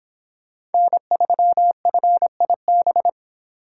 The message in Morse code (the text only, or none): N3FIB